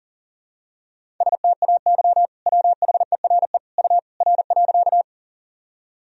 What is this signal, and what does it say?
Morse code, heard as a keyed tone, about 40 wpm, 700 Hz. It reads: STAY WHERE U R.
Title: Sentences from Top 500 Words and Abbreviations 40wpm